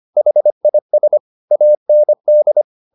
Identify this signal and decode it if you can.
Morse code, heard as a keyed tone, about 25 wpm, 600 Hz. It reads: HIS AND